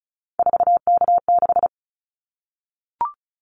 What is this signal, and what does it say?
Morse code, heard as a keyed tone, about 34 wpm, 700 Hz. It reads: 4K6
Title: Callsign Prefixes for Top 100 DXCC - 35wpm